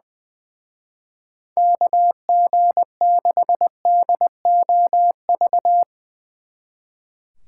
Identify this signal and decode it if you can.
Morse code, heard as a keyed tone, about 22 wpm, 700 Hz. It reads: KG6DOV
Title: Speed Racing - USA 2x3 Call Signs 20wpm